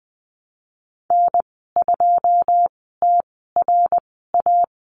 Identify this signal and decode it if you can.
Morse code, heard as a keyed tone, about 21 wpm, spaced as for 15 wpm, 700 Hz. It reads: N2TRA
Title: 186 Lesson 30g - Letter b - Call Signs 20wpm